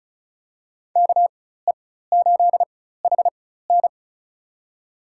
Morse code, heard as a keyed tone, about 36 wpm, spaced as for 16 wpm, 700 Hz. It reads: KE8HN
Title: USA Call Signs - 5 Characters - 4x Character Spacing - 35wpm